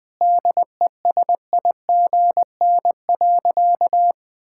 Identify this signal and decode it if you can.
Morse code, heard as a keyed tone, about 20 wpm, 700 Hz. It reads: DESIGN.
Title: Sentences from Top 1000 Words - Review Part 6 - 20wpm